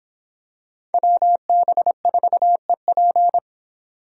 Morse code, heard as a keyed tone, about 26 wpm, 700 Hz. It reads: WB4EP